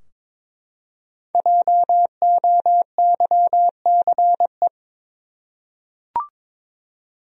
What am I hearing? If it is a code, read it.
Morse code, heard as a keyed tone, about 23 wpm, 700 Hz. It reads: JOYCE
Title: USA Common Names - Max 5 Letters - 22wpm